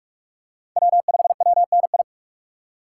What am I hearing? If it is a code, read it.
Morse code, heard as a keyed tone, about 41 wpm, 700 Hz. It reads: W5WNI
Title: USA Call Signs - 5 Characters - 3x Character Spacing - 45wpm